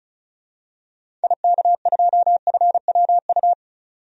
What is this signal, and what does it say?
Morse code, heard as a keyed tone, about 34 wpm, 700 Hz. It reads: IK2FWU
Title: DXCC-100 Call Signs - 6 Characters - 35wpm